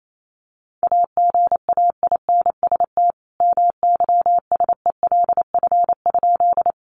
Morse code, heard as a keyed tone, about 28 wpm, 700 Hz. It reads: AGAINST MYSELF?
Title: Sentences from Top 1300 Words 28wpm